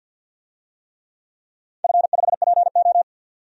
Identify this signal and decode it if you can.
Morse code, heard as a keyed tone, about 49 wpm, 700 Hz. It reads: F5PO
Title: DXCC-100 Call Signs - 4 Characters - 50wpm